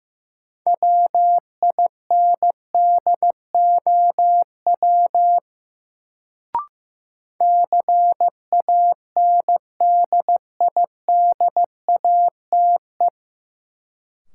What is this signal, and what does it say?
Morse code, heard as a keyed tone, about 16 wpm, 700 Hz. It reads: WINDOW CANDIDATE
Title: Sentences from Top 1600 Words - New Word Drill - 15wpm